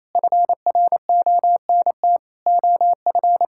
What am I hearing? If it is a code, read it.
Morse code, heard as a keyed tone, about 28 wpm, 700 Hz. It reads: FRONT OF
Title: Sentences from Top 1000 Words - Review Part 6 - 28wpm